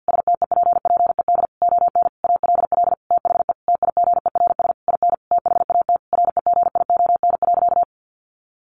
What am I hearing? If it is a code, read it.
Morse code, heard as a keyed tone, about 48 wpm, 700 Hz. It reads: STEPPED ON ALL THE TIGERS IN THAT REGION.
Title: A Sound Of Thunder 50wpm